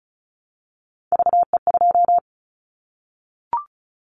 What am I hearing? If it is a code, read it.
Morse code, heard as a keyed tone, about 36 wpm, 700 Hz. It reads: VE2
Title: Callsign Prefixes for Top 50 DXCC - 35wpm